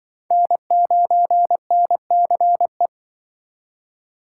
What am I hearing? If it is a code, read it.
Morse code, heard as a keyed tone, about 24 wpm, 700 Hz. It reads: N9NCE
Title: Speed Racing - USA 1x3 Call Signs 20wpm